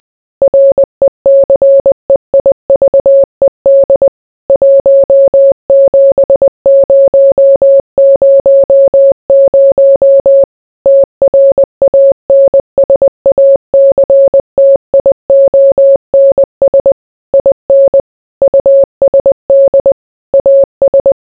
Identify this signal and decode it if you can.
Morse code, heard as a keyed tone, about 20 wpm, 550 Hz. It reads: RECEIVED 17000 TRANSACTIONS IN USD AS